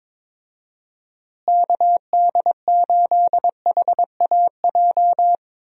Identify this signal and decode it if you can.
Morse code, heard as a keyed tone, about 22 wpm, 700 Hz. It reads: KD8HAJ
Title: USA Call Signs - 6 Characters - Repeated 3x - 22wpm